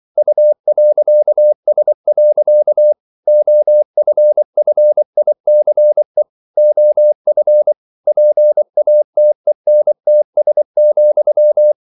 Morse code, heard as a keyed tone, about 24 wpm, 600 Hz. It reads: U.S. OFFICE OF PATENTS,
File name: CW / Practice / EnglishQuotations0424